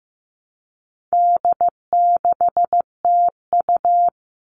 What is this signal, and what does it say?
Morse code, heard as a keyed tone, about 15 wpm, 700 Hz. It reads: D6TU
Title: All Cut Numbers - 4-Digit 15wpm